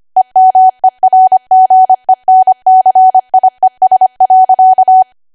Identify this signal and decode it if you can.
Morse code, heard as a keyed tone, about 25 wpm, 750 Hz. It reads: EMERGENCIES.
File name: Archive / 25 WPM / 25WPM